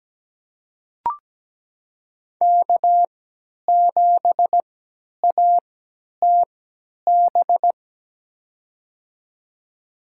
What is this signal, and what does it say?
Morse code, heard as a keyed tone, about 17 wpm, spaced as for 10 wpm, 700 Hz. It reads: K7ATB
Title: USA Call Signs - 5 Characters - 3x Character Spacing - 17wpm